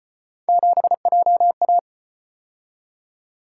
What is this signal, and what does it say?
Morse code, heard as a keyed tone, about 34 wpm, 700 Hz. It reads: 7JA